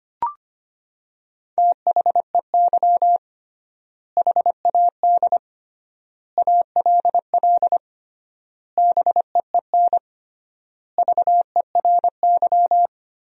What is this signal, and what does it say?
Morse code, heard as a keyed tone, about 25 wpm, 700 Hz. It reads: THEY HAD ALL BEEN VERY
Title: 3x Word-Spacing - Sentences from Top 200 Words - 25wpm